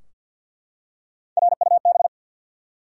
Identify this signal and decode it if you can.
Morse code, heard as a keyed tone, about 47 wpm, 700 Hz. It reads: RUB
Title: Three Letter Words - 50wpm